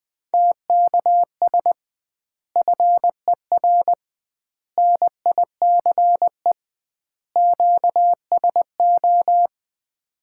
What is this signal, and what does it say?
Morse code, heard as a keyed tone, about 21 wpm, 700 Hz. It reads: TKS FER NICE QSO
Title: Common QSO Phrases - 2x Word-Spacing - 20wpm